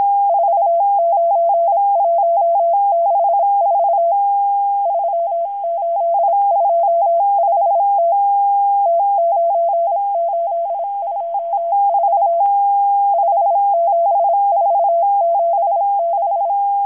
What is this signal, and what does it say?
Morse code, heard as a keyed tone, about 26 wpm, 680 Hz. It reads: 49164 3825T T9824 57476
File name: HM02- -0437utc-7351khz